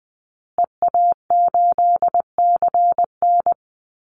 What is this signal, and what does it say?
Morse code, heard as a keyed tone, about 20 wpm, 700 Hz. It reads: EA8CN
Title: Call Signs Repeated 2x - 20wpm